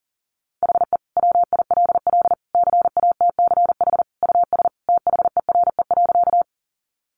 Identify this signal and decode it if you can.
Morse code, heard as a keyed tone, about 40 wpm, 700 Hz. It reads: HE WILL CATCH US THERE.
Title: Sentences from Top 1500 Words 40wpm